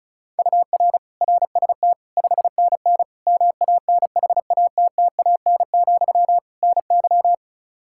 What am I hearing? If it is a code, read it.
Morse code, heard as a keyed tone, about 34 wpm, 700 Hz. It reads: UR RST 5NN MANHATTAN, NY
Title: Practice QSO - Exchange 1 - Beginning - 35wpm